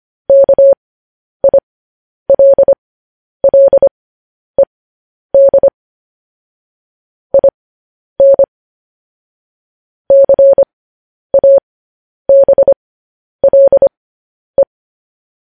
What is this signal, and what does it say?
Morse code, heard as a keyed tone, about 25 wpm, spaced as for 10 wpm, 550 Hz. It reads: KILLED IN CABLE
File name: podcast / News headlines-1x-25x10